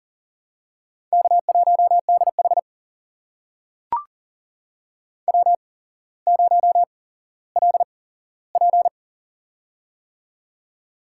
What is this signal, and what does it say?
Morse code, heard as a keyed tone, about 40 wpm, spaced as for 17 wpm, 700 Hz. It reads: K1DH W0LP
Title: USA Call Signs - 4 Characters - 8x Character Spacing - 40wpm